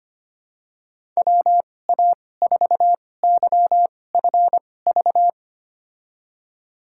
Morse code, heard as a keyed tone, about 25 wpm, spaced as for 18 wpm, 700 Hz. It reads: WA4YFV
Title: USA Call Signs - 6 Characters - 2x Character Spacing - 25wpm